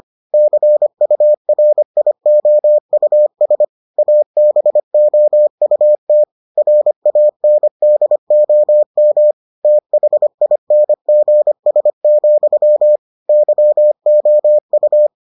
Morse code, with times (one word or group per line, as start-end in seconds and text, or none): 0.3-3.6 CURIOUS
4.0-6.2 ABOUT
6.6-9.3 RANDOM
9.6-13.0 THINGS,
13.3-15.1 YOU